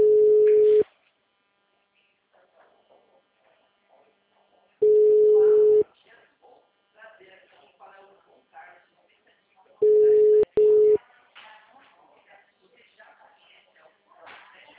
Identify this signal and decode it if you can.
Ringback tone (Europe, 425 Hz), with speech.